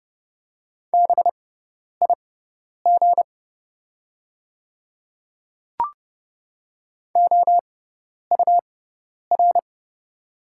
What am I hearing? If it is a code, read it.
Morse code, heard as a keyed tone, about 30 wpm, spaced as for 10 wpm, 700 Hz. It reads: BIG OUR
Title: 6x Character Spacing - Top 200 Words - 30wpm